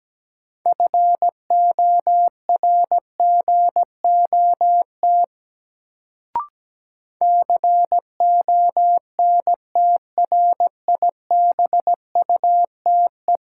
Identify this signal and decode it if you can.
Morse code, heard as a keyed tone, about 17 wpm, 700 Hz. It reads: FORGOT CONTRIBUTE
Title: Sentences from Top 1400 Words - New Word Drill - 17wpm